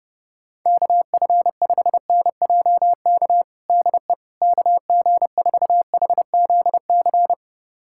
Morse code, heard as a keyed tone, about 30 wpm, 700 Hz. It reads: KF5NJK DE KG4HZC
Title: USA Callsign DE Callsign - 6 Characters - 30wpm